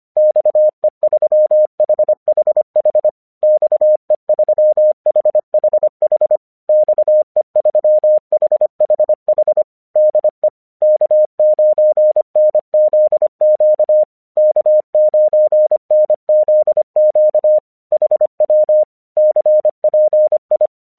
Morse code, heard as a keyed tone, about 25 wpm, 610 Hz. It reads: XE3HHH XE3HHH XE3HHH DE K9NZQ K9NZQ HW CPI
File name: 1 English 25wpm / AQC30